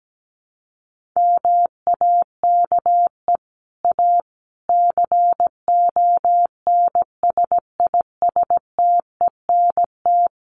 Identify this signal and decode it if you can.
Morse code, heard as a keyed tone, about 17 wpm, 700 Hz. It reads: MAKE A CONSISTENT
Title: Sentences from Top 1500 Words 17wpm